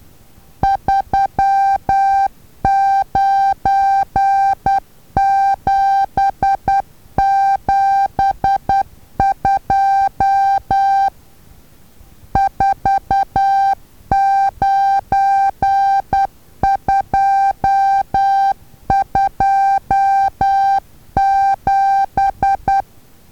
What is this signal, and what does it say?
Morse code, heard as a keyed tone, about 10 wpm, 790 Hz. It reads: 39772 49227
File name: Taviro HA6VA A